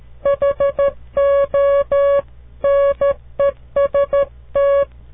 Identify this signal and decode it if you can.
Morse code, heard as a keyed tone, about 13 wpm, 560 Hz. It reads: HONEST
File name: CW Learning Programs / farnsworth / unit 2